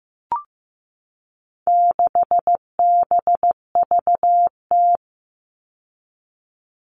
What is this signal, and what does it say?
Morse code, heard as a keyed tone, about 15 wpm, 700 Hz. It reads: 6BVT
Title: All Cut Numbers - 4-Digit 15wpm